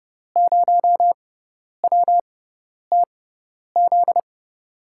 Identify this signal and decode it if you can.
Morse code, heard as a keyed tone, about 32 wpm, spaced as for 10 wpm, 700 Hz. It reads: 0WTZ
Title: USA Call Signs - 6 Characters - 6x Character Spacing - 30wpm